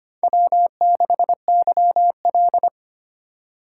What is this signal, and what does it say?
Morse code, heard as a keyed tone, about 25 wpm, 700 Hz. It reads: W6YL
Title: USA Call Signs - 4x Character Spacing - 25wpm